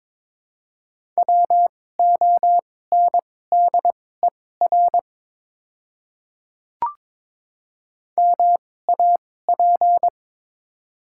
Morse code, heard as a keyed tone, about 22 wpm, spaced as for 16 wpm, 700 Hz. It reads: WONDER MAP
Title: 2x Character Spacing - Top 500 Words - 22wpm